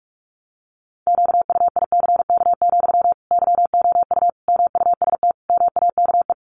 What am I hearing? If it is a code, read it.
Morse code, heard as a keyed tone, about 44 wpm, 700 Hz. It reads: QUICK, YOU MUST MAKE
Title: Sentences from Top 500 Words 45wpm